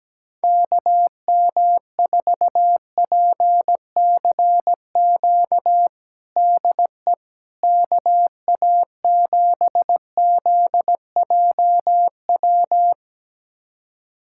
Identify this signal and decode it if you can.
Morse code, heard as a keyed tone, about 17 wpm, 700 Hz. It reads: KM4PCQ DE KA7ZJW
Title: USA Callsign DE Callsign - 6 Characters - 17wpm